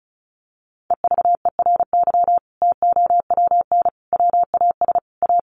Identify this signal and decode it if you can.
Morse code, heard as a keyed tone, about 34 wpm, 700 Hz. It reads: EVERY TOWN WAS A
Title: Sentences from Top 1500 Words 35wpm